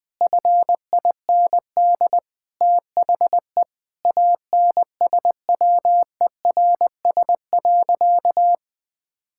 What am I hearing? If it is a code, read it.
Morse code, heard as a keyed tone, about 20 wpm, 700 Hz. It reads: FIND THE ANSWERS.
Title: Sentences with 5-Word Phrases - Spoken Beginning - 20wpm